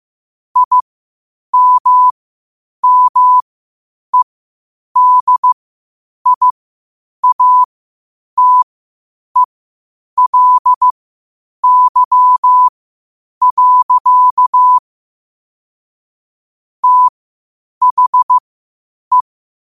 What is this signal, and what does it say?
Morse code, heard as a keyed tone, about 15 wpm, spaced as for 8 wpm, 1000 Hz. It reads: IMMEDIATELY. THE
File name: qotdfiles / 8wpm